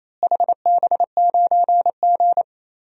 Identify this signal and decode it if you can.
Morse code, heard as a keyed tone, about 28 wpm, 700 Hz. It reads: HB9G
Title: DXCC-100 Call Signs - 4 Characters - Repeated 3x - 28wpm